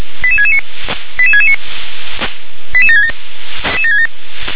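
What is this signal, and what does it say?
Five-tone selective call (ZVEI), heard as quick runs of five tones, tones of about 70 ms. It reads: 88669 88669 88767
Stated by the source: Five-tone ZVEI selective call received on an SDR.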